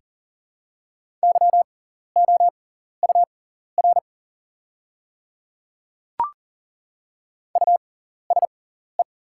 Morse code, heard as a keyed tone, about 40 wpm, spaced as for 14 wpm, 700 Hz. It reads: YOUR USE
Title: 6x Character Spacing - Top 100 Words - 40wpm